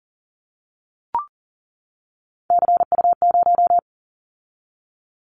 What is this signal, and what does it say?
Morse code, heard as a keyed tone, about 38 wpm, 700 Hz. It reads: CU0
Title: Callsign Prefixes for Top 50 DXCC - 40wpm